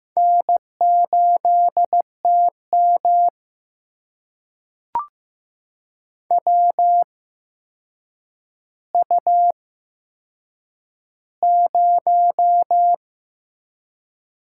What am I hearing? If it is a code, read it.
Morse code, heard as a keyed tone, about 15 wpm, spaced as for 7 wpm, 700 Hz. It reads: N8TM WU0
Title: USA Call Signs - 4 Characters - 8x Character Spacing - 15wpm